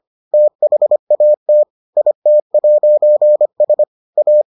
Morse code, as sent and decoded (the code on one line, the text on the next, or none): - .... .- - / .. - .----. ... / .-
THAT IT'S A